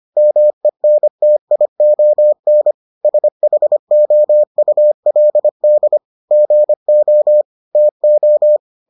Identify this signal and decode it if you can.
Morse code, heard as a keyed tone, about 25 wpm, 600 Hz. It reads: MENTION SHOULD GO TO